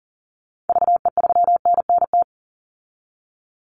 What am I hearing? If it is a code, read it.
Morse code, heard as a keyed tone, about 40 wpm, 700 Hz. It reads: VE3NNT
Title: Call Signs Repeated 3x - 40wpm